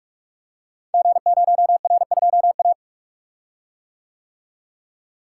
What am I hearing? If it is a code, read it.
Morse code, heard as a keyed tone, about 44 wpm, 700 Hz. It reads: G0R2A
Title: Code Groups - 5 Letter:Numbers - 45wpm